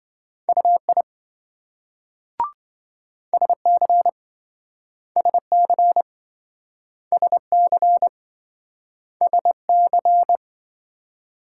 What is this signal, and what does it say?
Morse code, heard as a keyed tone, about 28 wpm, 700 Hz. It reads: UI SC SC SC SC